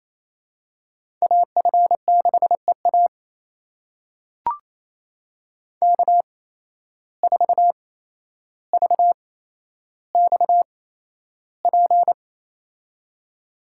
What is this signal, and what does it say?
Morse code, heard as a keyed tone, about 28 wpm, spaced as for 12 wpm, 700 Hz. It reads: AF6EA K4VXP
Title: USA Call Signs - 5 Characters - 8x Character Spacing - 28wpm